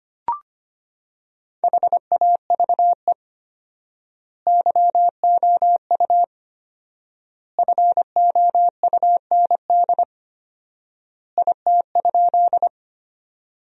Morse code, heard as a keyed tone, about 25 wpm, 700 Hz. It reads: HAVE YOU FOUND IT?
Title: 4x Word-Spacing - Sentences from Top 300 Words - 25wpm